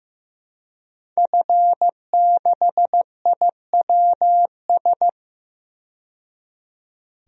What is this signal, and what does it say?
Morse code, heard as a keyed tone, about 15 wpm, 700 Hz. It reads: F6IWS
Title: DXCC-100 Call Signs - 5 Characters - 15wpm